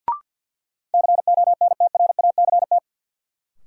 Morse code, heard as a keyed tone, about 49 wpm, 700 Hz. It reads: CONTRACT
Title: Sentences from Top 1400 Words - New Word Drill Alt - 50wpm